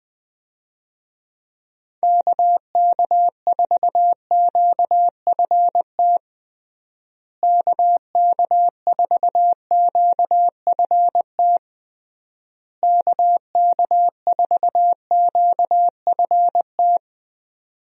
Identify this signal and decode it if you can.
Morse code, heard as a keyed tone, about 20 wpm, 700 Hz. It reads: KK4QFT KK4QFT KK4QFT
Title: USA Call Signs - 6 Characters - Repeated 3x - 20wpm